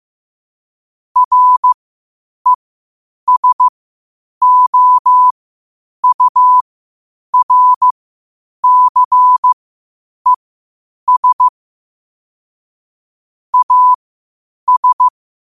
Morse code, heard as a keyed tone, about 15 wpm, spaced as for 8 wpm, 1000 Hz. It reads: RESOURCES AS
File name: qotdfiles / 8wpm